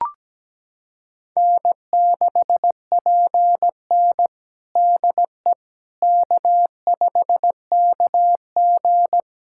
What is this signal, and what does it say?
Morse code, heard as a keyed tone, about 17 wpm, 700 Hz. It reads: N6PN DE K5KG
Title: Practice QSO - Exchange 4 - Ending - 17wpm